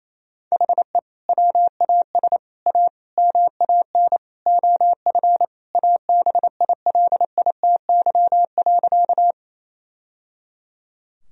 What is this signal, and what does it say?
Morse code, heard as a keyed tone, about 28 wpm, 700 Hz. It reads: HE WAS A MAN OF ABILITY.
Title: Sentences from Top 1000 Words - Review Part 3 - 28wpm